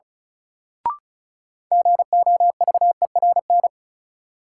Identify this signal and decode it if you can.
Morse code, heard as a keyed tone, about 36 wpm, 700 Hz. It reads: GOVERN
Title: Sentences from Top 1000 Words - Review Part 2 - New Word Drill - 35wpm